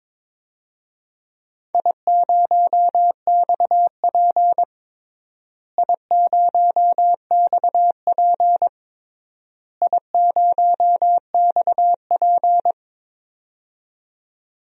Morse code, heard as a keyed tone, about 22 wpm, 700 Hz. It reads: I0XP I0XP I0XP
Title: DXCC-100 Call Signs - 4 Characters - Repeated 3x - 22wpm